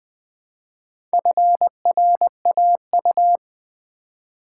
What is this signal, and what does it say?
Morse code, heard as a keyed tone, about 20 wpm, 700 Hz. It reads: FRAU
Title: Build A Word - 1 Letter At A Time - 5-Letter Words - 20wpm